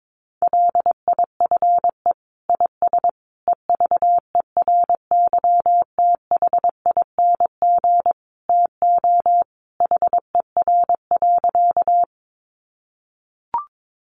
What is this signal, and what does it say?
Morse code, heard as a keyed tone, about 22 wpm, 700 Hz. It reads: LIFE IS EVERYTHING TO HER.
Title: Sentences from Top 700 Words 22wpm